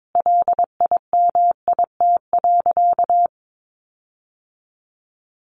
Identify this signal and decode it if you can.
Morse code, heard as a keyed tone, about 22 wpm, 700 Hz. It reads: LIMIT.
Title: Sentences from Top 1500 Words 22wpm